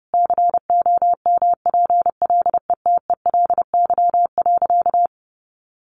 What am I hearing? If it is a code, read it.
Morse code, heard as a keyed tone, about 30 wpm, 700 Hz. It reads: COMPLETELY.
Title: Sentences from Top 1400 Words 30wpm